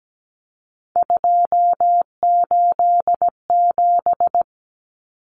Morse code, heard as a keyed tone, about 17 wpm, 700 Hz. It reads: 287
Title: Numbers - 3-Digit 17wpm